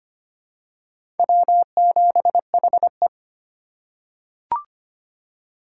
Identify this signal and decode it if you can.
Morse code, heard as a keyed tone, about 25 wpm, 700 Hz. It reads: W7HE
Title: USA Call Signs - 4 Characters - 3x Character Spacing - 25wpm